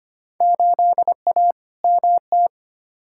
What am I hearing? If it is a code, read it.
Morse code, heard as a keyed tone, about 25 wpm, 700 Hz. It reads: 8A MT
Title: Field Day Exchanges Encore - 25wpm